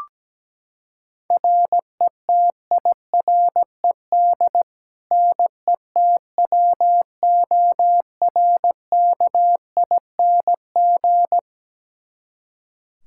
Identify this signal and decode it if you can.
Morse code, heard as a keyed tone, about 17 wpm, 700 Hz. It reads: RETIRED NETWORKING
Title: Practice QSO - Exchange 3 - More Info - 17wpm